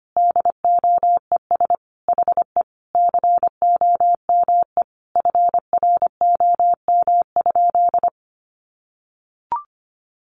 Morse code, heard as a keyed tone, about 25 wpm, 700 Hz. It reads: DOES HE COME FROM?
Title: Sentences from Top 300 Words 25wpm